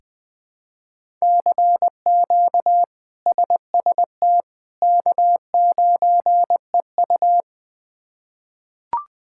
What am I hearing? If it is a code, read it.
Morse code, heard as a keyed tone, about 20 wpm, 700 Hz. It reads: CQ SST K9EU
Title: K1USN SST QSO Practice - 20wpm